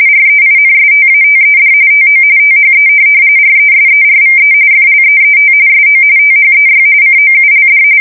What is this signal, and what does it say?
RTTY, about 45.45 baud. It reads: REINSTATED TO THE RESERVE FLEET SO THEY COULD
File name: fsk170 / USS-New-Jersey